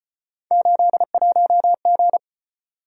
Morse code, heard as a keyed tone, about 34 wpm, 700 Hz. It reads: 81G